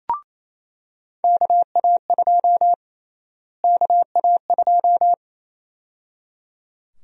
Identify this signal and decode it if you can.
Morse code, heard as a keyed tone, about 28 wpm, 700 Hz. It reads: KA2 KA2
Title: USA Call Sign Prefixes - Repeated 2x - 28wpm